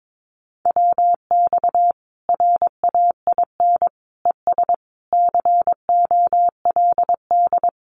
Morse code, heard as a keyed tone, about 22 wpm, 700 Hz. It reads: WX RAIN ES COLD
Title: Common QSO Phrases 22wpm